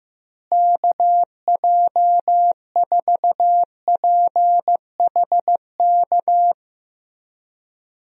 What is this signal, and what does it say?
Morse code, heard as a keyed tone, about 15 wpm, 700 Hz. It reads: KJ4PHK
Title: USA Call Signs - 6 Characters - Repeated 3x - 15wpm